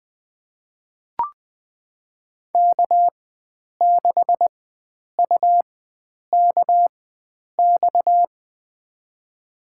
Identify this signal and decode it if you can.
Morse code, heard as a keyed tone, about 20 wpm, spaced as for 9 wpm, 700 Hz. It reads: K6UKX
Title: USA Call Signs - 5 Characters - 4x Character Spacing - 20wpm